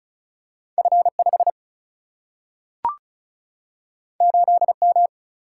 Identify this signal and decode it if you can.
Morse code, heard as a keyed tone, about 36 wpm, 700 Hz. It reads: F 5 8 M
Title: Single-Letter Single-Number 35wpm